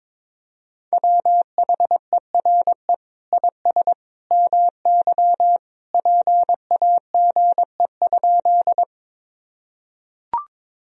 Morse code, heard as a keyed tone, about 22 wpm, 700 Hz. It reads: WHERE IS MY PAGE?
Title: Sentences from Top 1000 Words - Review Part 1 - 22wpm